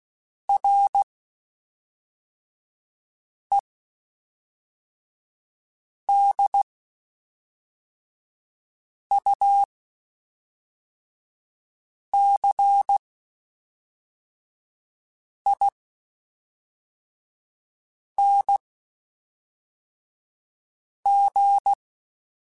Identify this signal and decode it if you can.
Morse code, heard as a keyed tone, about 16 wpm, spaced as for 3 wpm, 800 Hz. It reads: REDUCING
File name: QSOs / QSOs-mp3s / WD0FBT 1@5